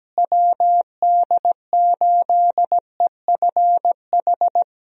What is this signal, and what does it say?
Morse code, heard as a keyed tone, about 17 wpm, 700 Hz. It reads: WD8EFH
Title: USA Call Signs - 6 Characters - Repeated 3x - 17wpm